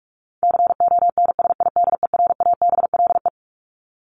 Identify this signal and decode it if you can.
Morse code, heard as a keyed tone, about 44 wpm, 700 Hz. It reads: CONSIDERABLE
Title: LY Suffix - Top 100 Words with Base Words 45wpm